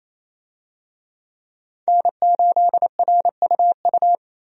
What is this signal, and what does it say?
Morse code, heard as a keyed tone, about 27 wpm, 700 Hz. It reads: N8RUU
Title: USA Call Signs - 5 Characters - 28wpm